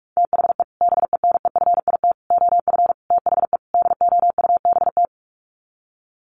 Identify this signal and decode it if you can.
Morse code, heard as a keyed tone, about 44 wpm, 700 Hz. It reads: THE BENEFIT OF THE DOUBT
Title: Sets of 5 Words Encore 45wpm